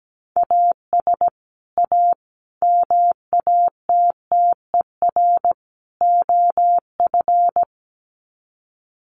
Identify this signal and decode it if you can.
Morse code, heard as a keyed tone, about 17 wpm, 700 Hz. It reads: AS A MATTER OF
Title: Sets of 4 Words 17wpm